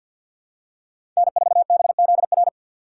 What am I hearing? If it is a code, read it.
Morse code, heard as a keyed tone, about 53 wpm, 700 Hz. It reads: N4BZR
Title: USA Call Signs - 5 Characters - 2x Character Spacing - 50wpm